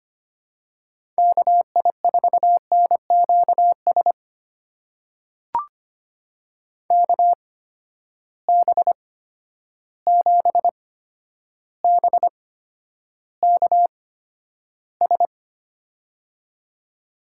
Morse code, heard as a keyed tone, about 25 wpm, spaced as for 11 wpm, 700 Hz. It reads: KI4NQS KB7BKS
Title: USA Call Signs - 6 Characters - 8x Character Spacing - 25wpm